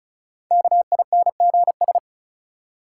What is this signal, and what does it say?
Morse code, heard as a keyed tone, about 34 wpm, 700 Hz. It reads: KINGS
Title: Sentences from Top 1000 Words - Review Part 6 - New Word Drill - 35wpm